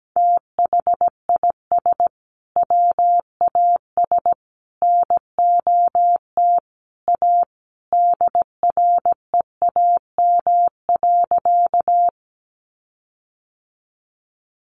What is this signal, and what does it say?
Morse code, heard as a keyed tone, about 17 wpm, 700 Hz. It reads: THIS WAS NOT A DREAM.
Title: Sentences from Top 1500 Words 17wpm